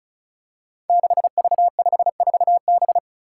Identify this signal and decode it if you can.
Morse code, heard as a keyed tone, about 36 wpm, 700 Hz. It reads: 6V54B
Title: Five Challenging Letter-Number Combinations 35wpm